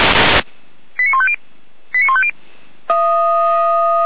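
Five-tone selective call (ZVEI), heard as quick runs of five tones, tones of about 70 ms. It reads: 89150 89150
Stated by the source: ZVEI five-tone selective call 89020 received on SDR.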